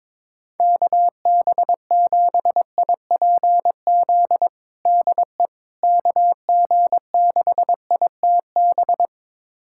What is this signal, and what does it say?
Morse code, heard as a keyed tone, about 22 wpm, 700 Hz. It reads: KB7IPZ DE KG6ITB
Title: USA Callsign DE Callsign 22wpm